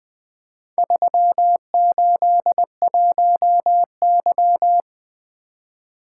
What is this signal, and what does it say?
Morse code, heard as a keyed tone, about 20 wpm, 700 Hz. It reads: 381Y
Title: Four Letter-Number Combinations - 20wpm